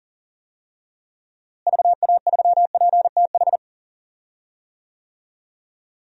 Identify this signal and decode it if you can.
Morse code, heard as a keyed tone, about 40 wpm, 700 Hz. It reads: VA3PTH
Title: DXCC-100 Call Signs - 6 Characters - Repeated 2x - 40wpm